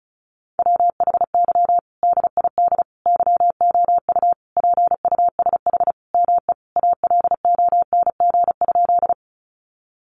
Morse code, heard as a keyed tone, about 34 wpm, 700 Hz. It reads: WHY DID YOU PUSH ME ALONG?
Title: Sentences from Top 1300 Words 35wpm